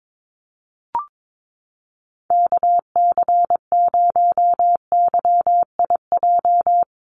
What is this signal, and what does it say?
Morse code, heard as a keyed tone, about 22 wpm, 700 Hz. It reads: KC0YIJ
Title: USA Call Signs 22wpm